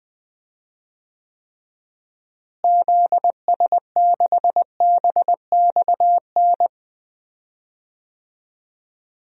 Morse code, heard as a keyed tone, about 20 wpm, 700 Hz. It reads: ZS6BXN
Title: DXCC-100 Call Signs - 6 Characters - Repeated 3x - 20wpm